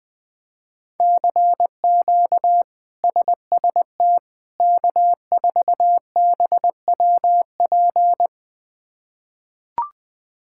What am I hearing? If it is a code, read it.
Morse code, heard as a keyed tone, about 20 wpm, 700 Hz. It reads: CQ SST K4BWP
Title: K1USN SST QSO Practice - 20wpm